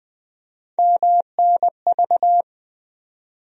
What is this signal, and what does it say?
Morse code, heard as a keyed tone, about 20 wpm, 700 Hz. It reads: MNV
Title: USA Call Sign Postfixes - 3 Letters 20wpm